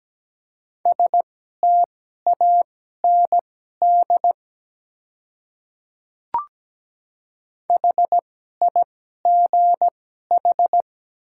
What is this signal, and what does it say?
Morse code, heard as a keyed tone, about 17 wpm, spaced as for 12 wpm, 700 Hz. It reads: STAND HIGH
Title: 2x Character Spacing - Top 300 Words - 17wpm